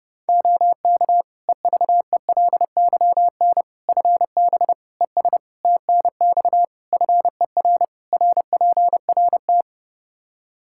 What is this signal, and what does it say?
Morse code, heard as a keyed tone, about 30 wpm, 700 Hz. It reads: OK EVELYN FB ES TNX FER RPRT
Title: Practice QSO - Exchange 2 - Info - 30wpm